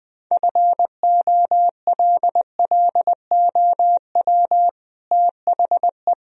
Morse code, heard as a keyed tone, about 20 wpm, 700 Hz. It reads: FOLLOW THE
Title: Sentences from Top 1000 Words - Review Part 4 - 20wpm